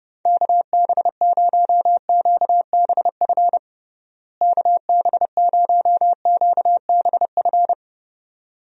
Morse code, heard as a keyed tone, about 30 wpm, 700 Hz. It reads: KB0QBF KB0QBF
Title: USA Call Signs - 6 Characters - Repeated 2x - 30wpm